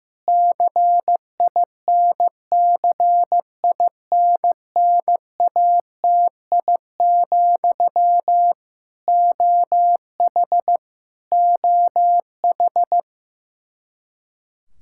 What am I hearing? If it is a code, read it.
Morse code, heard as a keyed tone, about 15 wpm, 700 Hz. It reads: CINCINNATI, OH OH
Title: Practice QSO - Exchange 1 - Beginning - 15wpm